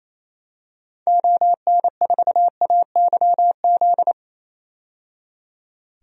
Morse code, heard as a keyed tone, about 28 wpm, 700 Hz. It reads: ON4AYZ
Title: DXCC-100 Call Signs - 6 Characters - 28wpm